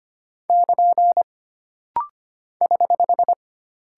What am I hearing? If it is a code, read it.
Morse code, heard as a keyed tone, about 25 wpm, 700 Hz. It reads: <KN> <HH>
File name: Single Punctuation-Prosign - Rapid-Fire - 25wpm